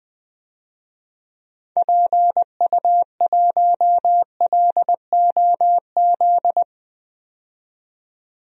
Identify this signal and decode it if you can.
Morse code, heard as a keyed tone, about 20 wpm, 700 Hz. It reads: PU1LOZ
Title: DXCC-100 Call Signs - 6 Characters - 20wpm